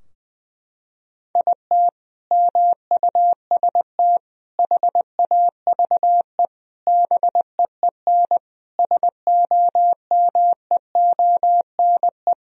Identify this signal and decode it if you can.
Morse code, heard as a keyed tone, about 21 wpm, 700 Hz. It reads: IT MUST HAVE BEEN SOMEONE
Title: Sentences from Top 1000 Words - Review Part 4 - 20wpm